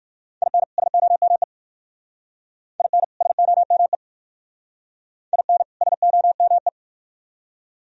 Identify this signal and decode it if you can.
Morse code, heard as a keyed tone, about 51 wpm, 700 Hz. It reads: IN SOME IN SOME IN SOME
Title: Speed-Racing - Sets of 2 Words 45wpm